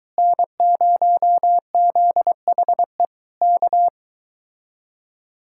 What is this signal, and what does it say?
Morse code, heard as a keyed tone, about 22 wpm, 700 Hz. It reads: N0ZHE K